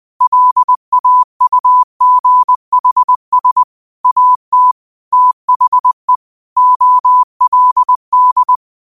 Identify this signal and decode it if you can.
Morse code, heard as a keyed tone, about 20 wpm, 1000 Hz. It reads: LAUGHS AT THE OLD
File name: qotdfiles / 20wpm